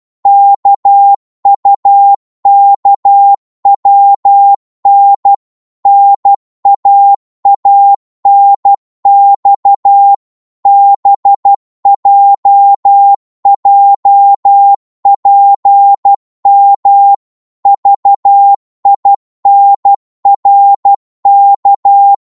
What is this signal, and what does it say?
Morse code, heard as a keyed tone, about 12 wpm, 800 Hz. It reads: KUKWN NAANX BJJPM VINRK
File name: Morse / ltr-12wpm